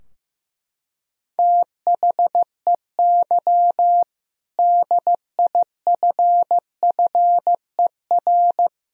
Morse code, heard as a keyed tone, about 15 wpm, 700 Hz. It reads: THEY DIFFER
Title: Sentences from Top 1000 Words - Review Part 2 - 15wpm